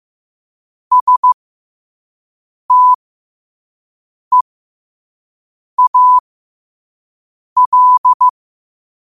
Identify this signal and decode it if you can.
Morse code, heard as a keyed tone, about 15 wpm, spaced as for 5 wpm, 1000 Hz. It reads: STEAL
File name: qotdfiles / 5wpm